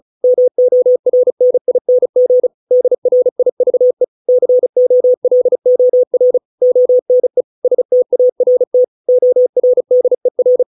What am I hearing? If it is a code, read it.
Morse code, heard as a keyed tone, about 34 wpm, 500 Hz. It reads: MORNING DRIVE COLOR ONE START ORDER